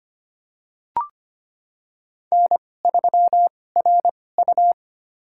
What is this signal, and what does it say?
Morse code, heard as a keyed tone, about 25 wpm, spaced as for 18 wpm, 700 Hz. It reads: N3RU
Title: USA Call Signs - 4 Characters - 2x Character Spacing - 25wpm